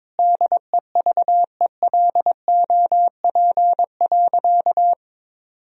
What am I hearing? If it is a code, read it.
Morse code, heard as a keyed tone, about 21 wpm, 700 Hz. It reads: DEVELOP.
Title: Sentences from Top 1000 Words - Review Part 2 - 22wpm